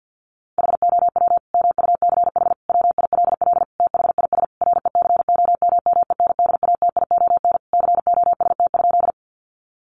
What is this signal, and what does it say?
Morse code, heard as a keyed tone, about 50 wpm, 700 Hz. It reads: HOW MUCH WILL THIS RECOMMENDATION COST?
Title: Sentences from Top 1400 Words 50wpm